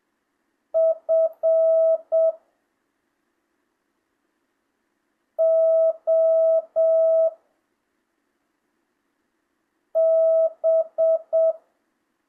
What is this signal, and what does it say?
Morse code, heard as a keyed tone, about 7 wpm, 660 Hz. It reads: F O B
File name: jh8cbh / 2025cw